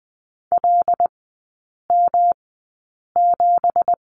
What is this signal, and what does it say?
Morse code, heard as a keyed tone, about 20 wpm, spaced as for 8 wpm, 700 Hz. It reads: LM7
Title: 172 Lesson 28e - Number 9 - Three Characters 20wpm